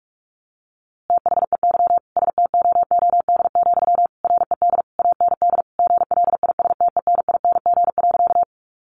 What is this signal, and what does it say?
Morse code, heard as a keyed tone, about 44 wpm, 700 Hz. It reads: THEY STOOD, RED AND GLISTENING.
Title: A Sound Of Thunder 45wpm